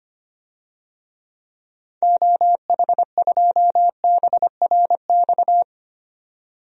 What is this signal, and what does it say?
Morse code, heard as a keyed tone, about 25 wpm, 700 Hz. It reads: OH2BRX
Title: DXCC-100 Call Signs - 6 Characters - 25wpm